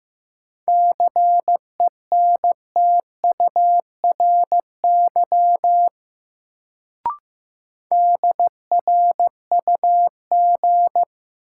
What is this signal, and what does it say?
Morse code, heard as a keyed tone, about 15 wpm, 700 Hz. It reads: CENTURY DRUG
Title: Sets of 5 Words - New Word Drill - 15wpm